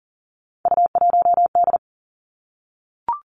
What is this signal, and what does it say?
Morse code, heard as a keyed tone, about 41 wpm, 700 Hz. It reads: U1D
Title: Three Challenging Letter-Number Combinations 40wpm